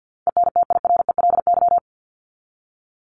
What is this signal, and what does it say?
Morse code, heard as a keyed tone, about 50 wpm, 700 Hz. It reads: ENTIRELY
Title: LY Suffix - Top 100 Words with Base Words 50wpm